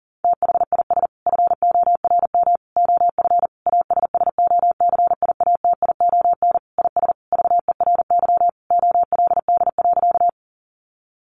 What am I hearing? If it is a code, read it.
Morse code, heard as a keyed tone, about 38 wpm, 700 Hz. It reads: THIS FORM OF ASSOCIATION IS VERY OLD.
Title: Sentences from Top 1100 Words 40wpm